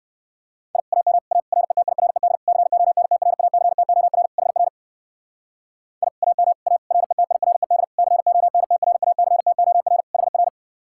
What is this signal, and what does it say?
Morse code, heard as a keyed tone, about 54 wpm, 700 Hz. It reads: I AM A RETIRED CONTRACTOR HR I AM A RETIRED CONTRACTOR HR
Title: Speed-Racing Common QSO Phrases - 45wpm